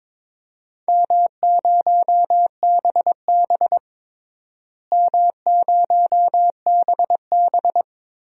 Morse code, heard as a keyed tone, about 22 wpm, 700 Hz. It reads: M0BB M0BB
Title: DXCC-100 Call Signs - 4 Characters - Repeated 3x - 22wpm